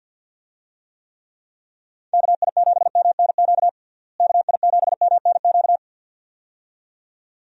Morse code, heard as a keyed tone, about 51 wpm, 700 Hz. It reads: KI7MNQ KI7MNQ
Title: USA Call Signs - 6 Characters - Repeated 2x - 50wpm